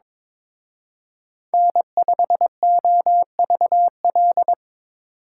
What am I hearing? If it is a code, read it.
Morse code, heard as a keyed tone, about 26 wpm, 690 Hz. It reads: N5OVL
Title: Speed Racing - USA Call Signs 22wpm